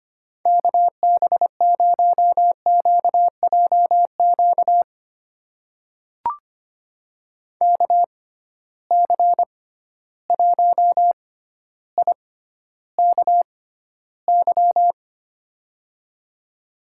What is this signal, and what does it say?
Morse code, heard as a keyed tone, about 25 wpm, spaced as for 13 wpm, 700 Hz. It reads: KB0QJQ KC1IKY
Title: USA Call Signs - 6x Character Spacing - 25wpm